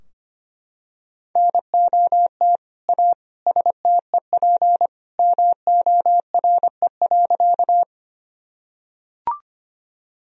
Morse code, heard as a keyed tone, about 25 wpm, 700 Hz. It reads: NOT A STEP MORE.
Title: Sentences from Top 1000 Words - Review Part 2 - 25wpm